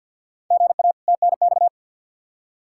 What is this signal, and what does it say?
Morse code, heard as a keyed tone, about 51 wpm, 700 Hz. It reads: GA TNX
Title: Practice QSO - Exchange 1 - Beginning - 50wpm